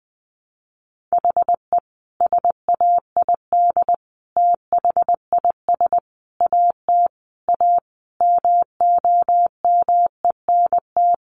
Morse code, heard as a keyed tone, about 20 wpm, 700 Hz. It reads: HE SAID THIS AT A MOMENT
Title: Sentences from Top 1000 Words 20wpm